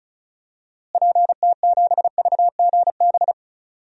Morse code, heard as a keyed tone, about 36 wpm, 700 Hz. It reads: PT7VGB
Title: DXCC-100 Call Signs - 6 Characters - Repeated 2x - 35wpm